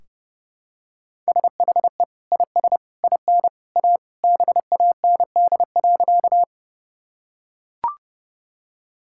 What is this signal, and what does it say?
Morse code, heard as a keyed tone, about 32 wpm, 700 Hz. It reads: SHE IS IN A BAND.
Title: Sentences from Top 1600 Words 30wpm